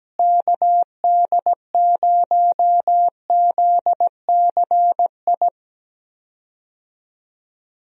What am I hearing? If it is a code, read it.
Morse code, heard as a keyed tone, about 17 wpm, 700 Hz. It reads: KD0ZCI
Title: USA Call Signs - 6 Characters - Repeated 3x - 17wpm